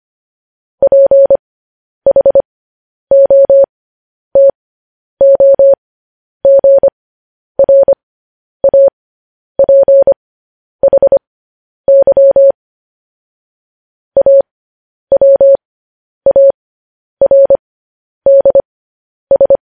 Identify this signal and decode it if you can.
Morse code, heard as a keyed tone, about 25 wpm, spaced as for 10 wpm, 550 Hz. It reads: PHOTOGRAPHY AWARDS
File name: podcast / News headlines-1x-25x10